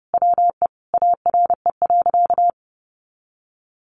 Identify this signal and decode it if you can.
Morse code, heard as a keyed tone, about 30 wpm, 700 Hz. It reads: WE ARE.
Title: Sentences from Top 200 Words 30wpm